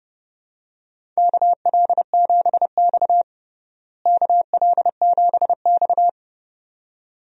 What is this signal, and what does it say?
Morse code, heard as a keyed tone, about 32 wpm, 700 Hz. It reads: KL7X KL7X
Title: DXCC-100 Call Signs - 4 Characters - Repeated 2x - 30wpm